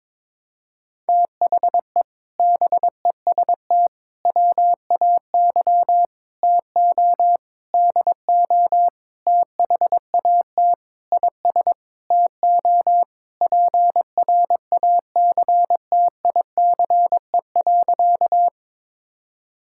Morse code, heard as a keyed tone, about 22 wpm, 700 Hz. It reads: THE BEST WAY TO DO THAT IS TO PRACTICE.
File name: practice / 22wpm / Sentences with 8-Word Phrases - 22wpm